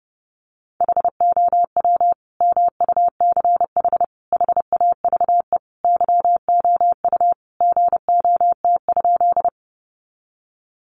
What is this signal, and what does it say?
Morse code, heard as a keyed tone, about 30 wpm, 700 Hz. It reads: HOW MUCH HAVE YOU GOT?
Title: Sentences from Top 300 Words 30wpm